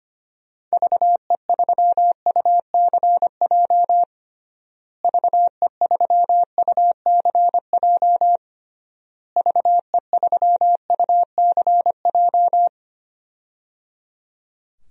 Morse code, heard as a keyed tone, about 25 wpm, 700 Hz. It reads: VE3UCJ VE3UCJ VE3UCJ
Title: DXCC-100 Call Signs - 6 Characters - Repeated 3x - 25wpm